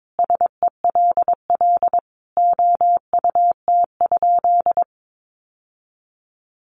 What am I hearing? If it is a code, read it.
Morse code, heard as a keyed tone, about 22 wpm, 700 Hz. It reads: SELL OUT?
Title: Sentences from Top 1100 Words 22wpm